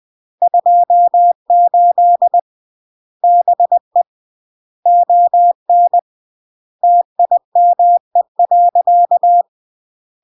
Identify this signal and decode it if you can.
Morse code, heard as a keyed tone, about 20 wpm, 700 Hz. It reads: 28 BE ON TIME.
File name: CW / jodorowsky / 82 commandments 20-20